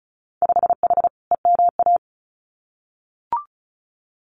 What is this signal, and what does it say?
Morse code, heard as a keyed tone, about 36 wpm, 700 Hz. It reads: 5H EMA
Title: Winter Field Day Exchanges Encore 35wpm